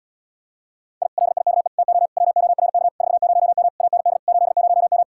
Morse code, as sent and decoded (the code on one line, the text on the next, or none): .. / .-.. .. -.- . / - --- / .-- --- .-. -.- / ....- ----- -- / .- -. -.. / ---.. ----- --
I LIKE TO WORK 40M AND 80M